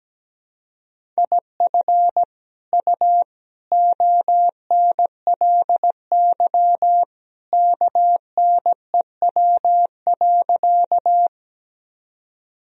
Morse code, heard as a keyed tone, about 17 wpm, 700 Hz. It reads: IF U ONLY KNEW.
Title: Sentences from Top 400 Words and Abbreviations 17wpm